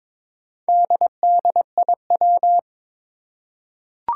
Morse code, heard as a keyed tone, about 22 wpm, 700 Hz. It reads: DDIW
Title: Four Challenging Letter-Number Combinations 22wpm